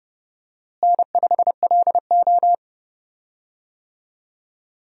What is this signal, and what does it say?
Morse code, heard as a keyed tone, about 30 wpm, 700 Hz. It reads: N5LO
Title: USA Call Signs - 4 Characters - Repeated 2x - 30wpm